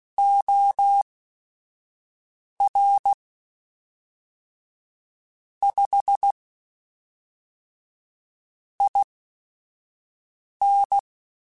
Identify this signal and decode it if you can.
Morse code, heard as a keyed tone, about 16 wpm, spaced as for 5 wpm, 800 Hz. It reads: OR 5 IN
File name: Technical Subjects / Technical Subjects-mp3s / AND Gate @5